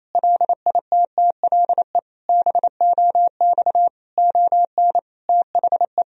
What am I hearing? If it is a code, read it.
Morse code, heard as a keyed tone, about 28 wpm, 700 Hz. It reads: LITTLE BOX ON THE
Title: Sentences from Top 1000 Words - Review Part 2 - 28wpm